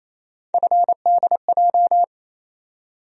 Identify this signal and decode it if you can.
Morse code, heard as a keyed tone, about 28 wpm, 700 Hz. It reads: FDJ
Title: USA Call Sign Postfixes - 3 Letters 28wpm